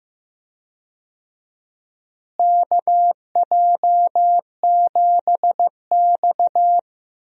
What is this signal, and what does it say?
Morse code, heard as a keyed tone, about 15 wpm, 700 Hz. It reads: KJ7X
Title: USA Call Signs - 4 Characters - Repeated 3x - 15wpm